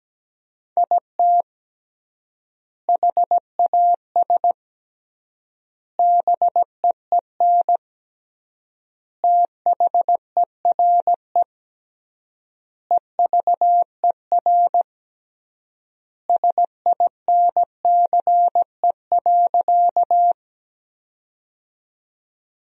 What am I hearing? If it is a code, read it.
Morse code, heard as a keyed tone, about 17 wpm, 700 Hz. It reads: IT HAS BEEN THERE EVER SINCE.
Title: 3x Word-Spacing - Sentences from Top 400 Words - 17wpm